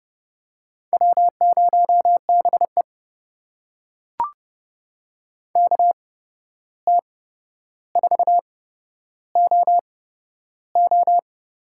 Morse code, heard as a keyed tone, about 30 wpm, spaced as for 12 wpm, 700 Hz. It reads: W0BE KT4OO
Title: USA Call Signs - 8x Character Spacing - 30wpm